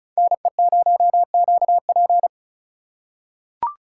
Morse code, heard as a keyed tone, about 36 wpm, 700 Hz. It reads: NE0QP
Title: USA Call Signs - 4x Character Spacing - 35wpm